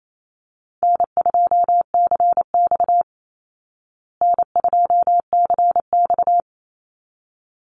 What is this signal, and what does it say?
Morse code, heard as a keyed tone, about 28 wpm, 700 Hz. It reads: N2CX N2CX
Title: Call Signs Repeated 3x - 28wpm